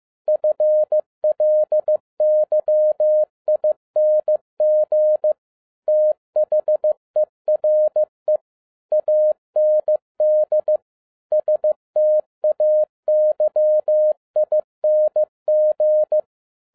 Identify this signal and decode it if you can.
Morse code, heard as a keyed tone, about 15 wpm, 610 Hz. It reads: FLYING THERE AND STAYING
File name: 2011-07-mp3 / n12 T8XO 15wpm